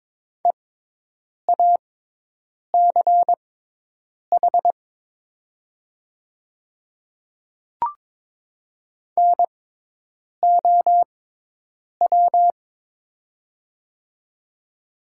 Morse code, heard as a keyed tone, about 22 wpm, spaced as for 8 wpm, 700 Hz. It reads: EACH NOW
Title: 6x Character Spacing - Top 100 Words - 22wpm